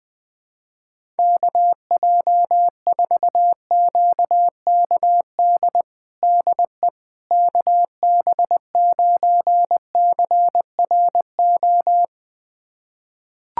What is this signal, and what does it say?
Morse code, heard as a keyed tone, about 20 wpm, 700 Hz. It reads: KJ4QKD DE KB9CRO
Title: USA Callsign DE Callsign - 6 Characters - 20wpm